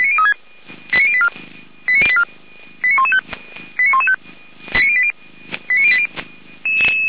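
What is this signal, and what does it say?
Five-tone selective call (ZVEI), heard as quick runs of five tones, tones of about 80 ms. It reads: 89084 89084 89115 89115 89080
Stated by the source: ZVEI five-tone call received on SDR.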